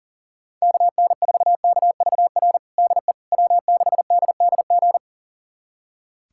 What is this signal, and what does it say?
Morse code, heard as a keyed tone, about 40 wpm, 700 Hz. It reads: KN4KVR DE W6DDG
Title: USA Callsign DE Callsign 40wpm